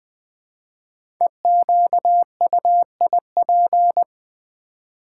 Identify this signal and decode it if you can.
Morse code, heard as a keyed tone, about 20 wpm, 700 Hz. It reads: EQUIP
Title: 5 Letter Words - First 3 Letters Given - 20wpm